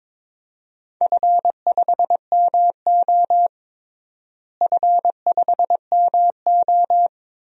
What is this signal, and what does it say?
Morse code, heard as a keyed tone, about 21 wpm, 700 Hz. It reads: F5MO F5MO
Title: DXCC-100 Call Signs - 4 Characters - Repeated 2x - 22wpm